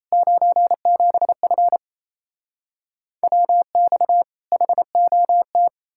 Morse code, heard as a keyed tone, about 37 wpm, 700 Hz. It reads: 97F WX HOT